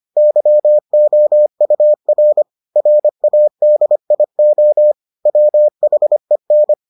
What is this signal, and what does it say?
Morse code, heard as a keyed tone, about 25 wpm, 600 Hz. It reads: YOUR RADIO WHEN